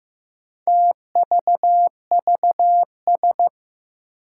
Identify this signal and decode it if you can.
Morse code, heard as a keyed tone, about 15 wpm, 700 Hz. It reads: TVVS
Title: Four Challenging Letter-Number Combinations 15wpm